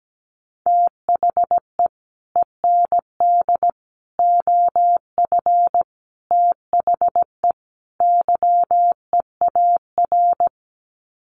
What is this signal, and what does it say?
Morse code, heard as a keyed tone, about 17 wpm, 700 Hz. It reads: THE END OF THE YEAR
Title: Sets of 5 Words 17wpm